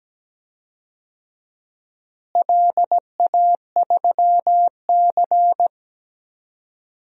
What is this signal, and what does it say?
Morse code, heard as a keyed tone, about 17 wpm, 700 Hz. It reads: LA3C
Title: DXCC-100 Call Signs - 4 Characters - Repeated 2x - 17wpm